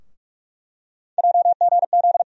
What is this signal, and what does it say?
Morse code, heard as a keyed tone, about 46 wpm, 700 Hz. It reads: JGZ
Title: USA Call Sign Postfixes - 3 Letters 45wpm